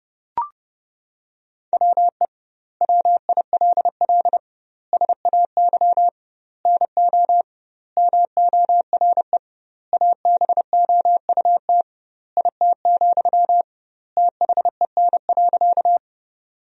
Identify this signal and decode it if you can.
Morse code, heard as a keyed tone, about 29 wpm, 700 Hz. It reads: WE WILL SAY NO MORE ABOUT IT, THEN.
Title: 2x Word-Spacing - Sentences from Top 200 Words - 30wpm